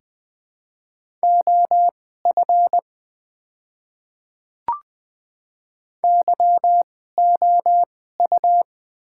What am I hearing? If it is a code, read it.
Morse code, heard as a keyed tone, about 20 wpm, spaced as for 14 wpm, 700 Hz. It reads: OF YOU
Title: 2x Character Spacing - Top 100 Words - 20wpm